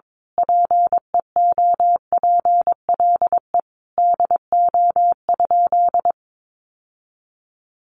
Morse code, heard as a keyed tone, about 22 wpm, 700 Hz. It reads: PEOPLE DO?
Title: Sentences from Top 1100 Words 22wpm